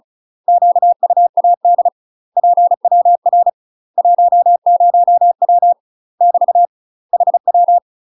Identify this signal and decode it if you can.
Morse code, heard as a keyed tone, about 36 wpm, 700 Hz. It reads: QUAD PWR 10W <BT> HW
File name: UK MASTER QSOs TRAINING / 35WPM / UK MASTER QSOs 0013